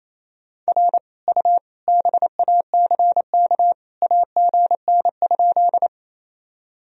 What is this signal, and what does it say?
Morse code, heard as a keyed tone, about 28 wpm, 700 Hz. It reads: R U BACK AGN?
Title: Sentences from Top 200 Words and Abbreviations 28wpm